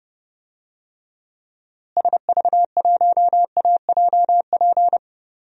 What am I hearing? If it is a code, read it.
Morse code, heard as a keyed tone, about 30 wpm, 700 Hz. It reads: SV1AJP
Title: DXCC-100 Call Signs - 6 Characters - Repeated 3x - 30wpm